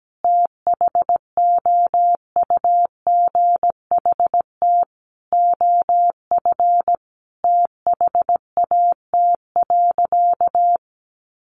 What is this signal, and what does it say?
Morse code, heard as a keyed tone, about 17 wpm, 700 Hz. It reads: THOUGHT OF THAT.
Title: Sentences from Top 300 Words 17wpm